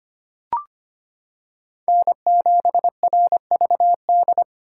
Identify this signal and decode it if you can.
Morse code, heard as a keyed tone, about 25 wpm, 700 Hz. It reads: N7RVD
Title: K1USN SST QSO Practice - 25wpm